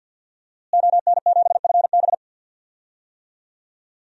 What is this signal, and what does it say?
Morse code, heard as a keyed tone, about 52 wpm, 700 Hz. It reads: ON7FB
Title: DXCC-100 Call Signs - 5 Characters - Repeated 3x - 50wpm